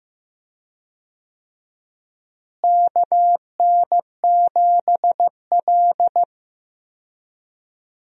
Morse code, heard as a keyed tone, about 15 wpm, 700 Hz. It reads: KN7L
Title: USA Call Signs - 4 Characters - Repeated 3x - 15wpm